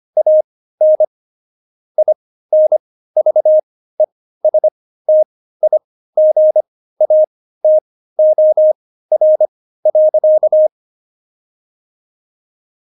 Morse code, heard as a keyed tone, about 25 wpm, spaced as for 15 wpm, 620 Hz. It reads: AN INVESTIGATOR.
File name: Jokes-25-char-15-eff